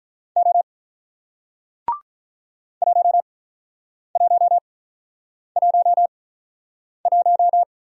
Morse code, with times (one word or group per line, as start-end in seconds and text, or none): 0.4-0.6 O
2.8-7.6 1111